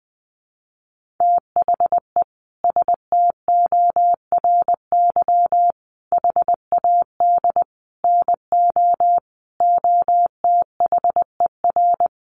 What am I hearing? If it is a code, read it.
Morse code, heard as a keyed tone, about 20 wpm, 700 Hz. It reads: THE STORY HAD NO OTHER
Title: Sentences from Top 1200 Words 20wpm